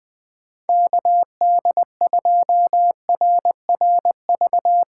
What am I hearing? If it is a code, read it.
Morse code, heard as a keyed tone, about 20 wpm, 700 Hz. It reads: KD2RRV
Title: USA Call Signs - 6x Character Spacing - 20wpm